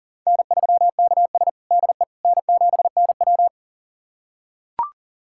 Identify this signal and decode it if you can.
Morse code, heard as a keyed tone, about 40 wpm, 700 Hz. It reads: N3KS DE N7NW
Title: USA Callsign DE Callsign - 4 Characters - 40wpm